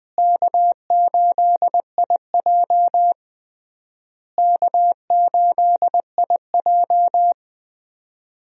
Morse code, heard as a keyed tone, about 21 wpm, 700 Hz. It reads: K8IJ K8IJ
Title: USA Call Signs - 4 Characters - Repeated 3x - 20wpm